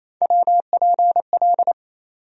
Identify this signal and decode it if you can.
Morse code, heard as a keyed tone, about 28 wpm, 700 Hz. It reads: WPL
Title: USA Call Sign Postfixes - 3 Letters 28wpm